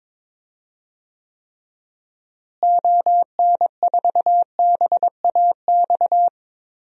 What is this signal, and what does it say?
Morse code, heard as a keyed tone, about 22 wpm, 700 Hz. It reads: ON4BAX
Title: DXCC-100 Call Signs - 6 Characters - Repeated 2x - 22wpm